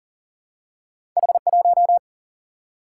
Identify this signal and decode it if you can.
Morse code, heard as a keyed tone, about 39 wpm, 700 Hz. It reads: H 1
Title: Single-Letter Single-Number 40wpm